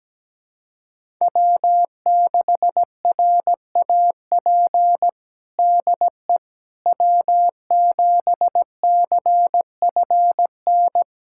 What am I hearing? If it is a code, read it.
Morse code, heard as a keyed tone, about 17 wpm, 700 Hz. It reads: W6RAP DE W7CFN
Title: USA Callsign DE Callsign - 5 Characters - 17wpm